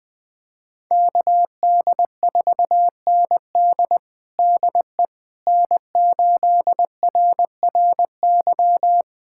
Morse code, heard as a keyed tone, about 20 wpm, 700 Hz. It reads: KD4ND DE N8RRY
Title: USA Callsign DE Callsign - 5 Characters - 20wpm